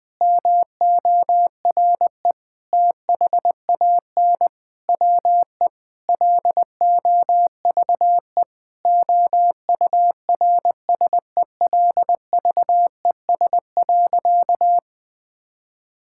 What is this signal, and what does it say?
Morse code, heard as a keyed tone, about 21 wpm, 700 Hz. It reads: MORE THAN WE LOVE OURSELVES.
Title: Sentences from Top 1600 Words 20wpm